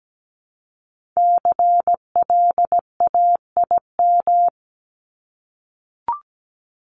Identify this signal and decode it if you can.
Morse code, heard as a keyed tone, about 17 wpm, 700 Hz. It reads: CLAIM
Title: Top 1251-1500 Words 17wpm